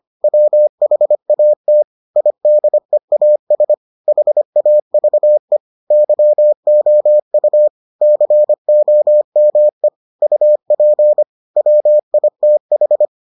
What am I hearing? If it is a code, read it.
Morse code, heard as a keyed tone, about 25 wpm, 600 Hz. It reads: WHAT IDEAS HAVE YOU COME UP WITH